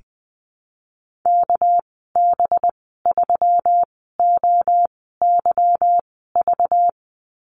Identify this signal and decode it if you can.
Morse code, heard as a keyed tone, about 19 wpm, spaced as for 14 wpm, 700 Hz. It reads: KB3OYV
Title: 238 Lesson 38g - Letter q - Call Signs 20wpm